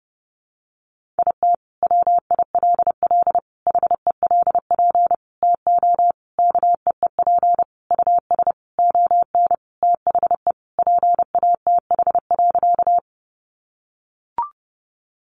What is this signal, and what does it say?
Morse code, heard as a keyed tone, about 30 wpm, 700 Hz. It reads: IT WILL HELP TO KEEP US ON THE PATH.
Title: Sentences from Top 1400 Words 30wpm